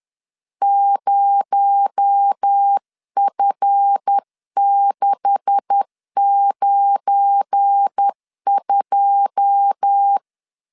Morse code, heard as a keyed tone, about 10 wpm, 800 Hz. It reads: 0F692
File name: LEZIONI / 9 286 10wpm 2735